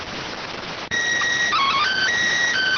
SSTV (slow-scan television), PD 290.